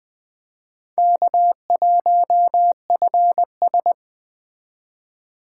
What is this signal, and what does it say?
Morse code, heard as a keyed tone, about 20 wpm, 700 Hz. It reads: K1FS
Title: Speed Racing - USA 1x2 Call Signs 17wpm